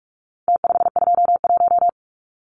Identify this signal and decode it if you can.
Morse code, heard as a keyed tone, about 44 wpm, 700 Hz. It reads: T521
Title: Numbers - Contest Style - 4-Digit 45wpm